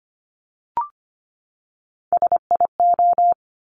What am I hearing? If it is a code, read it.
Morse code, heard as a keyed tone, about 25 wpm, 700 Hz. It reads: SIO
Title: Top Three Letter Combinations in Words 25wpm